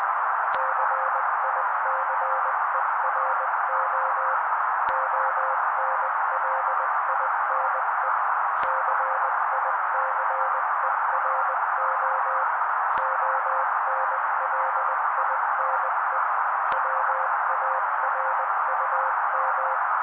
Morse code, heard as a keyed tone, about 20 wpm, 530 Hz. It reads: CICERO ONLINE CICERO ONLINE WARUM
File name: daily / 18wpm noise